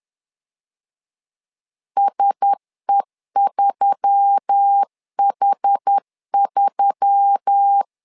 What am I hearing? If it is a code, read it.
Morse code, heard as a keyed tone, about 10 wpm, 800 Hz. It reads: SE3H3